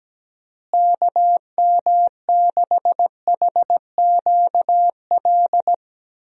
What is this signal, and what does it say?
Morse code, heard as a keyed tone, about 17 wpm, 700 Hz. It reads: KM6HQL
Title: USA Call Signs - 6 Characters - Repeated 3x - 17wpm